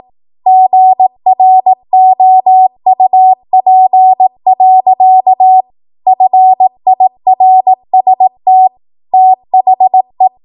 Morse code, heard as a keyed tone, about 18 wpm, 750 Hz. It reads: GROUP. FIRST THE
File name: Morse / 18WPM